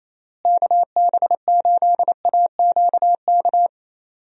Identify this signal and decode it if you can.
Morse code, heard as a keyed tone, about 27 wpm, 700 Hz. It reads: KB8AQK
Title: USA Call Signs - 6 Characters - Repeated 3x - 28wpm